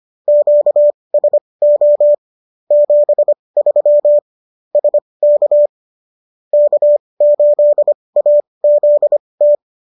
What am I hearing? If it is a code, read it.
Morse code, heard as a keyed tone, about 25 wpm, spaced as for 20 wpm, 590 Hz. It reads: QSO 73 SK K8AZT